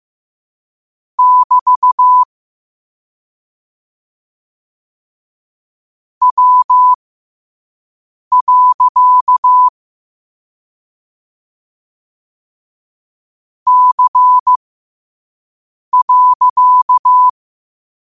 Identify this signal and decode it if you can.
Morse code, heard as a keyed tone, about 15 wpm, spaced as for 5 wpm, 1000 Hz. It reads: <BT> W. C.
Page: New quotes every day in morse code at 5 Words per minute.